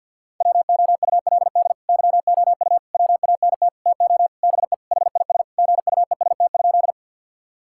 Morse code, heard as a keyed tone, about 50 wpm, 700 Hz. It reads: WOULD YOU WANT TO BE HIS GUEST?
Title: Sentences from Top 1600 Words 50wpm